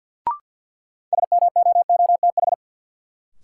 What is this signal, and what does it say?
Morse code, heard as a keyed tone, about 53 wpm, 700 Hz. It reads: SMOOTH
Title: Sentences from Top 1400 Words - New Word Drill - 50wpm